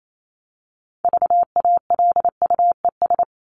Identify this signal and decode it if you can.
Morse code, heard as a keyed tone, about 28 wpm, 700 Hz. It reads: VALUES
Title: ES:S Suffix - Top 100 Words with Base Words 28wpm